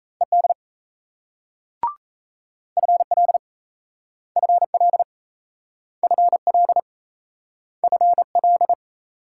Morse code, heard as a keyed tone, about 37 wpm, 700 Hz. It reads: ED FL FL FL FL